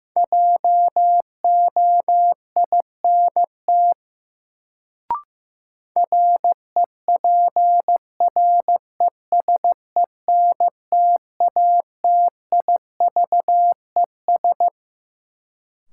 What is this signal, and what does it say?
Morse code, heard as a keyed tone, about 15 wpm, 700 Hz. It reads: JOINT REPRESENTATIVES
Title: Sets of 5 Words Encore - New Word Drill - 15wpm